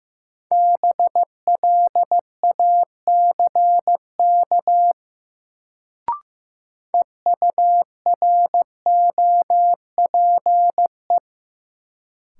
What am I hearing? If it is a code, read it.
Morse code, heard as a keyed tone, about 15 wpm, 700 Hz. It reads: BLACK EUROPE
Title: Sets of 5 Words Encore - New Word Drill - 15wpm